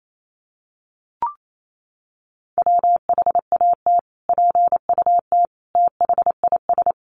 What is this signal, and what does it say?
Morse code, heard as a keyed tone, about 28 wpm, 700 Hz. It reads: WHAT PUT THIS
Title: Sentences from Top 600 Words 28wpm